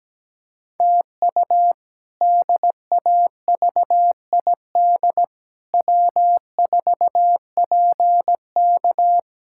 Morse code, heard as a keyed tone, about 17 wpm, 700 Hz. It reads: TU DAVID W4PK
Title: K1USN SST QSO Practice - 17wpm